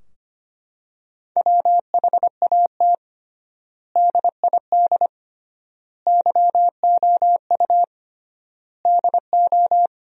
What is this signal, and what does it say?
Morse code, heard as a keyed tone, about 25 wpm, 700 Hz. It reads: WHAT DID YOU DO
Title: 3x Word-Spacing - Sentences from Top 100 Words - 25wpm